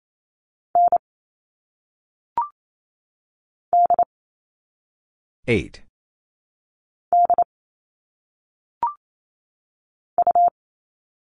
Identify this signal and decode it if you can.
Morse code, heard as a keyed tone, about 28 wpm, 700 Hz. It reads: N D D U
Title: All Cut Numbers - 1-Digit 28wpm